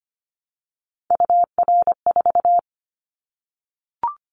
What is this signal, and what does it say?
Morse code, heard as a keyed tone, about 25 wpm, 700 Hz. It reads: UR4
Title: Callsign Prefixes for Top 25 DXCC - Repeated 3x - 25wpm